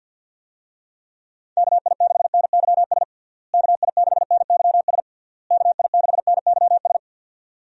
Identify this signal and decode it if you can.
Morse code, heard as a keyed tone, about 46 wpm, 700 Hz. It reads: KI6NYS KI6NYS KI6NYS
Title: USA Call Signs - 6 Characters - Repeated 3x - 50wpm